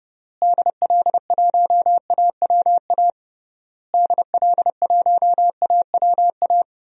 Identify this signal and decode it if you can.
Morse code, heard as a keyed tone, about 30 wpm, 700 Hz. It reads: DL1AWA DL1AWA
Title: DXCC-100 Call Signs - 6 Characters - Repeated 3x - 30wpm